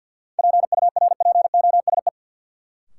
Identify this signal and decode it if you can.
Morse code, heard as a keyed tone, about 50 wpm, 700 Hz. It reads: PURPOSE
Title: Sentences from Top 800 Words - New Word Drill - 50wpm